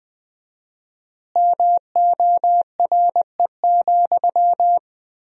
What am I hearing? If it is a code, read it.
Morse code, heard as a keyed tone, about 20 wpm, 700 Hz. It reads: MORE,
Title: 4x Word-Spacing - Sentences from Top 300 Words - 20wpm